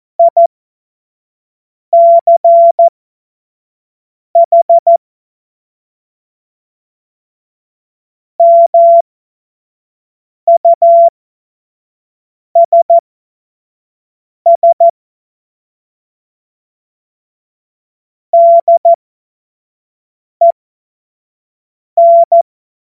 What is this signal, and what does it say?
Morse code, heard as a keyed tone, about 14 wpm, spaced as for 5 wpm, 680 Hz. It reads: ICH MUSS DEN
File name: Witz-12-August-2025-25-F70